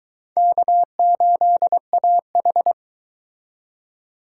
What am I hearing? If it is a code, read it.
Morse code, heard as a keyed tone, about 23 wpm, 700 Hz. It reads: K8AH